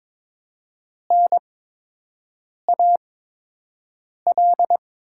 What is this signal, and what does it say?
Morse code, heard as a keyed tone, about 22 wpm, spaced as for 6 wpm, 700 Hz. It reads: NAL
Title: 8x Character Spacing - Top 500 Words - 22wpm